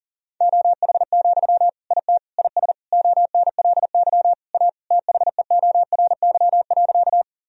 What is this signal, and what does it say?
Morse code, heard as a keyed tone, about 37 wpm, 700 Hz. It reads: OH, IT IS ONLY A THEORY.
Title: Sentences from Top 1000 Words - Review Part 3 - 40wpm